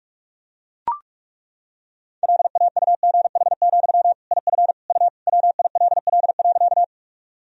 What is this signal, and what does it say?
Morse code, heard as a keyed tone, about 42 wpm, 700 Hz. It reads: LAUGH, IF U WILL.
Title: Sentences from Top 500 Words and Abbreviations 45wpm